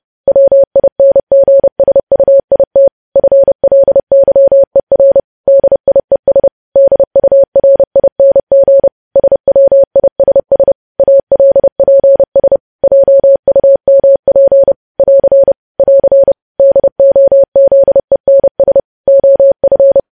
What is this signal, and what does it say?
Morse code, heard as a keyed tone, about 30 wpm, 550 Hz. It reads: WINGSUIT FLYER DIES DURING SWISS ALPS JUMP <AR> <AR> DOZENS OF
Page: news headlines in morse code at 30 wpm